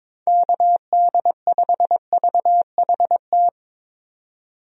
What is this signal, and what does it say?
Morse code, heard as a keyed tone, about 21 wpm, 700 Hz. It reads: KD5VHT
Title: USA Call Signs - 6 Characters - 22wpm